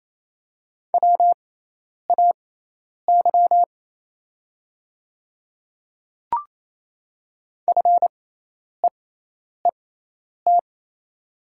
Morse code, heard as a keyed tone, about 28 wpm, spaced as for 10 wpm, 700 Hz. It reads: WAY FEET
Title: 6x Character Spacing - Top 300 Words - 28wpm